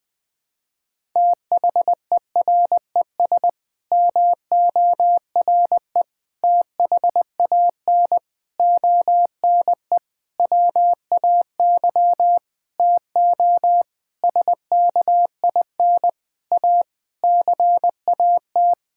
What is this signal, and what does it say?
Morse code, heard as a keyed tone, about 20 wpm, 700 Hz. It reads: THERES MORE THAN ONE WAY TO SKIN A CAT
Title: Sets of 9 Words - 20wpm